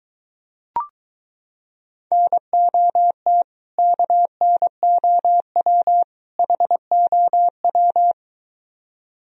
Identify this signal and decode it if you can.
Morse code, heard as a keyed tone, about 23 wpm, 700 Hz. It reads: NOT KNOW HOW